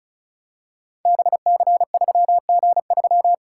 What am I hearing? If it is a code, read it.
Morse code, heard as a keyed tone, about 36 wpm, 700 Hz. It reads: BC3G3
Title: Code Groups - 5 Letter:Numbers - 35wpm